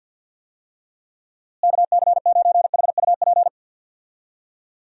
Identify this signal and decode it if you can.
Morse code, heard as a keyed tone, about 52 wpm, 700 Hz. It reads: KC9HUP
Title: USA Call Signs - 6 Characters - Repeated 3x - 50wpm